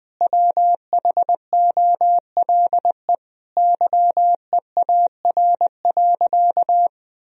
Morse code, heard as a keyed tone, about 21 wpm, 700 Hz. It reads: WHOLE YEAR.
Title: Sentences from Top 1000 Words - Review Part 1 - 20wpm